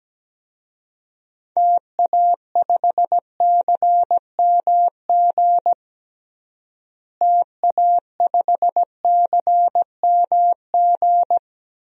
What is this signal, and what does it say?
Morse code, heard as a keyed tone, about 17 wpm, 700 Hz. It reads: TA5CMG TA5CMG
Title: DXCC-100 Call Signs - 6 Characters - Repeated 2x - 17wpm